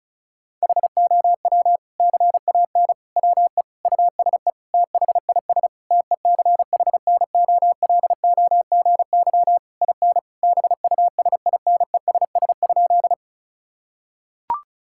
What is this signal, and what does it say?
Morse code, heard as a keyed tone, about 36 wpm, 700 Hz. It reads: HOW CAN WE USE THIS TECHNOLOGY IN BUSINESS?
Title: Sentences from Top 1000 Words - Review Part 5 - 35wpm